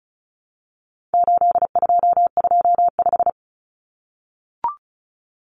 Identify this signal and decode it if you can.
Morse code, heard as a keyed tone, about 34 wpm, 700 Hz. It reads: 8225
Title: Numbers - 4-Digit 35wpm